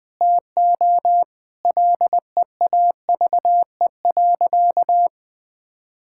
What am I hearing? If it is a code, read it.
Morse code, heard as a keyed tone, about 20 wpm, 700 Hz. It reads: TO LEAVE.
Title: Sentences with 4-Word Phrases - Spoken Beginning - 20wpm